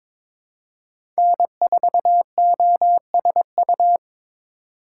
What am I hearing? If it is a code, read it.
Morse code, heard as a keyed tone, about 22 wpm, 700 Hz. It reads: N4OSU
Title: USA Call Signs - 5 Characters - 3x Character Spacing - 22wpm